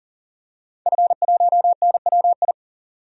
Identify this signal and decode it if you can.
Morse code, heard as a keyed tone, about 40 wpm, 700 Hz. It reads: F1NWI
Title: DXCC-100 Call Signs - 5 Characters - 40wpm